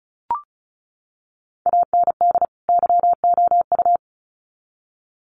Morse code, heard as a keyed tone, about 34 wpm, 700 Hz. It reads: AND YOU
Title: Sets of 2 Words 35wpm